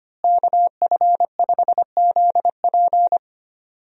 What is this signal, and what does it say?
Morse code, heard as a keyed tone, about 25 wpm, 700 Hz. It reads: KF5ZP
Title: USA Call Signs - 5 Characters - Repeated 3x - 25wpm